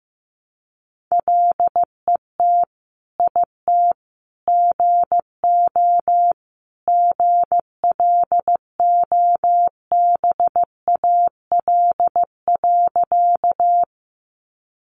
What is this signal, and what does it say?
Morse code, heard as a keyed tone, about 15 wpm, 700 Hz. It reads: LET IT GO GLOBAL.
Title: Sentences from Top 1200 Words 15wpm